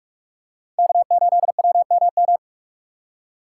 Morse code, heard as a keyed tone, about 44 wpm, 700 Hz. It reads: K8WMM